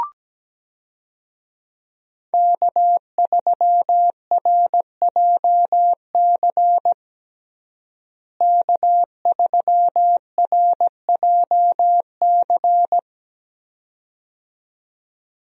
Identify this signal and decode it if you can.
Morse code, heard as a keyed tone, about 17 wpm, 700 Hz. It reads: K3RJC K3RJC
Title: USA Call Signs - 5 Characters - Repeated 2x - 17wpm